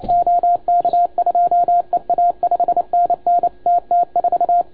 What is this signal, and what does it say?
Morse code, heard as a keyed tone, about 29 wpm, 690 Hz, with radio noise. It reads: OK2EA 5NN TT4
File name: Archiv / CUC287 / OK1FGD CUC 287